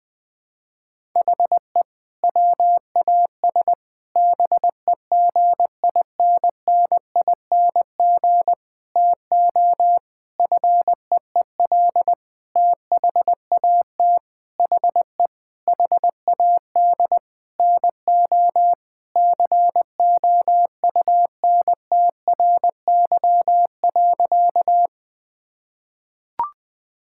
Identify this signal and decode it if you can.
Morse code, heard as a keyed tone, about 20 wpm, 700 Hz. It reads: HE WAS BEGINNING TO FEEL THAT HE HAD NO COUNTRY.
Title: Sentences from Top 1000 Words - Review Part 5 - 20wpm